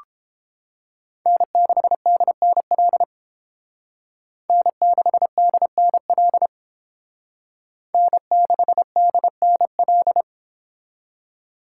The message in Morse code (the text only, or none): N6DNL N6DNL N6DNL